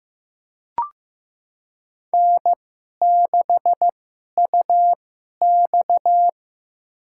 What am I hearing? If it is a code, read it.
Morse code, heard as a keyed tone, about 15 wpm, spaced as for 11 wpm, 700 Hz. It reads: N6UX
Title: USA Call Signs - 4 Characters - 2x Character Spacing - 15wpm